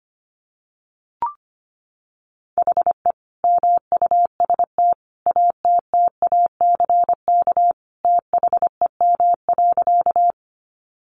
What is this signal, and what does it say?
Morse code, heard as a keyed tone, about 25 wpm, 700 Hz. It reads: HE MUST ATTACK THEM.
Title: Sentences from Top 1500 Words 25wpm